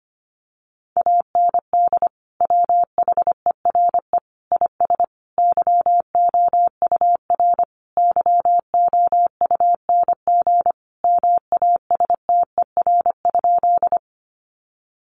Morse code, heard as a keyed tone, about 25 wpm, 700 Hz. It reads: AND WHERE IS YOUR YOUNG MASTER?
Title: Sentences from Top 1300 Words 25wpm